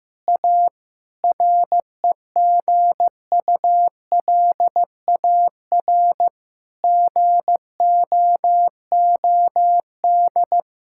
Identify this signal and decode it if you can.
Morse code, heard as a keyed tone, about 15 wpm, 700 Hz. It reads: A REGULAR GOOD
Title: Sentences from Top 1000 Words - Review Part 6 - 15wpm